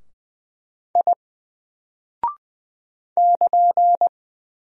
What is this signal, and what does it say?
Morse code, heard as a keyed tone, about 20 wpm, 700 Hz. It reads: I <KN>
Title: Single Character (Letter:Number:Punctuation:Prosign) Alt 20wpm